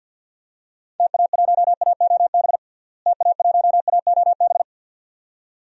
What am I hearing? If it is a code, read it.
Morse code, heard as a keyed tone, about 50 wpm, 700 Hz. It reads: TA1AOB TA1AOB
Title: DXCC-100 Call Signs - 6 Characters - Repeated 2x - 50wpm